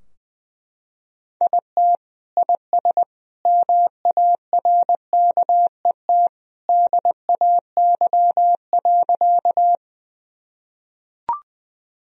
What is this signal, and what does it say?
Morse code, heard as a keyed tone, about 21 wpm, 700 Hz. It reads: IT IS MARKET DAY.
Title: Sentences from Top 1000 Words - Review Part 3 - 20wpm